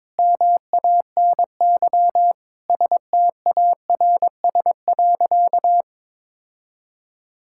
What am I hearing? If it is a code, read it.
Morse code, heard as a keyed tone, about 22 wpm, 700 Hz. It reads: MANY STARS.
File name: practice / 22wpm / Sentences with 2-Word Phrases - 22wpm